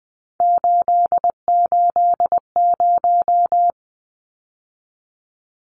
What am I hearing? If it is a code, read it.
Morse code, heard as a keyed tone, about 20 wpm, 700 Hz. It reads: 880
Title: Numbers - 3-Digit 20wpm